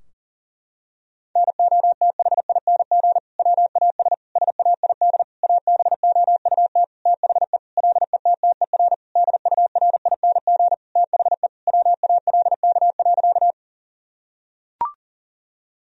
Morse code, heard as a keyed tone, about 37 wpm, 700 Hz. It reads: NOTHING WAS SAID ABOUT THE LETTER DURING THE WALK.
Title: Sentences from Top 1000 Words - Review Part 2 - 40wpm